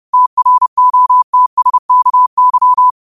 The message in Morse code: TROTSKY

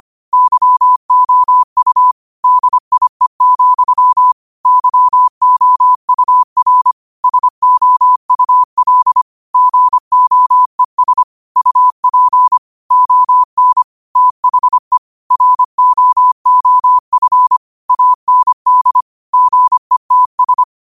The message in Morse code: YOU DIE, YOUR SOUL GOES UP ON THE ROOF AND GETS

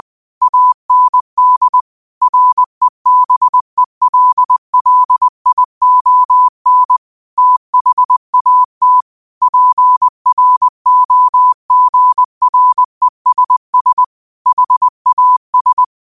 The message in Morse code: AND REBELLION THAT PROGRESS HAS

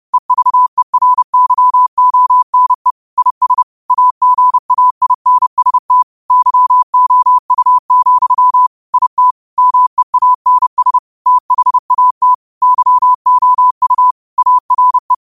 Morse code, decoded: EVERYONE IS AGAINST YOU, IT MEANS THAT YOU ARE

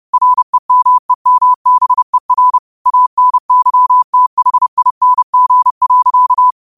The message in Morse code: REMEMBER ANYTHING.

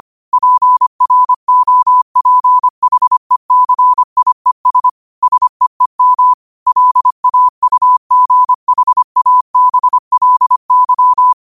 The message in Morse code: PROPHECIES SEEM LAUGHABLY

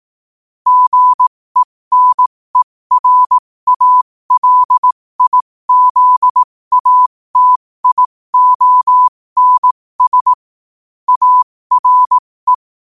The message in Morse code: GENERALIZATIONS ARE

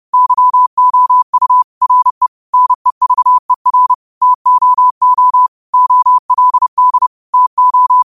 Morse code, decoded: YOU RE NEVER TOO OLD TO

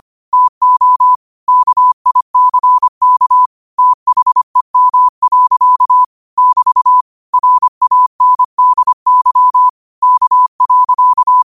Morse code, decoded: TO KICK THEM. <BT> RANDY K.

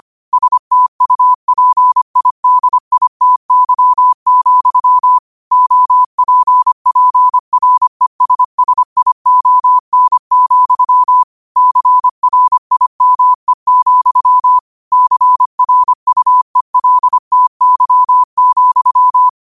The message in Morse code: STUPIDITY, OPPRESSION, CRIME, CRUELTY,

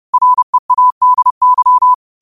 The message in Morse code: READY

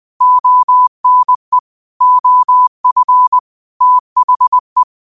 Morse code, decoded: ONE OF THE

